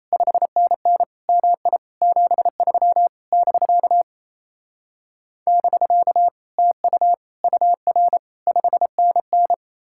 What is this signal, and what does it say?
Morse code, heard as a keyed tone, about 36 wpm, 700 Hz. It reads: 5NN MI 73 <BK> <BK> TU UR 5NN